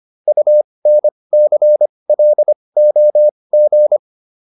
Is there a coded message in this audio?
Yes: Morse code from 0.3 to 4.0 s, heard as a keyed tone, about 25 wpm, spaced as for 20 wpm, 600 Hz. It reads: UNCLOG